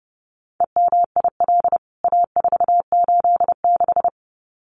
Morse code, heard as a keyed tone, about 30 wpm, 700 Hz. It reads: EMIL A486
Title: CWT Exchange 30wpm